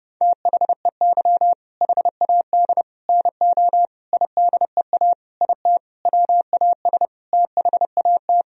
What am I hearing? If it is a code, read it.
Morse code, heard as a keyed tone, about 32 wpm, 700 Hz. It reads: THEY HAD NO IDEA IT WAS THAT
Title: Sentences from Top 1000 Words - Review Part 1 - 30wpm